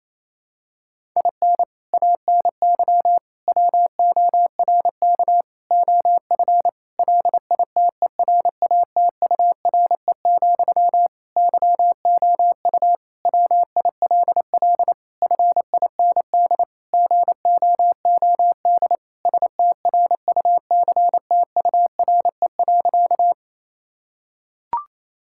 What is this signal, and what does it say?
Morse code, heard as a keyed tone, about 27 wpm, 700 Hz. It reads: IN ANY WORK OF LITERATURE, YOU WILL FIND GOOD STRUCTURE.
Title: Sentences from Top 1000 Words - Review Part 6 - 28wpm